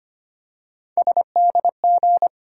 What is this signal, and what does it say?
Morse code, heard as a keyed tone, about 25 wpm, 700 Hz. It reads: SDG
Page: morse-code-ninja-practice-25wpm